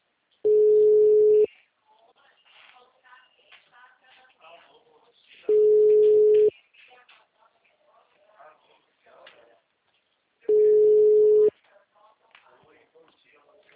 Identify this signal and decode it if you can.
Ringback tone (Europe, 425 Hz), with speech.